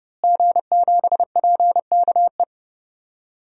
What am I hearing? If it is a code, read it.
Morse code, heard as a keyed tone, about 30 wpm, 700 Hz. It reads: G7PKE
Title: DXCC-100 Call Signs - 5 Characters - Repeated 3x - 30wpm